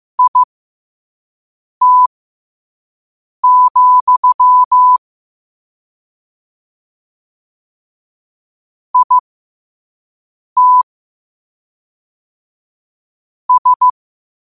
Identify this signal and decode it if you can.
Morse code, heard as a keyed tone, about 15 wpm, spaced as for 5 wpm, 1000 Hz. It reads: IT, IT S
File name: qotdfiles / 5wpm